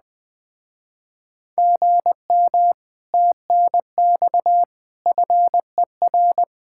Morse code, heard as a keyed tone, about 20 wpm, 700 Hz. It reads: GM TNX FER
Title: Speed-Racing Common QSO Phrases - 17wpm